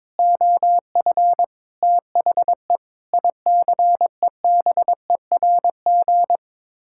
Morse code, heard as a keyed tone, about 22 wpm, 700 Hz. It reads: OF THE ICEBERG
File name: practice / 22wpm / Sets of 6 Words - 22wpm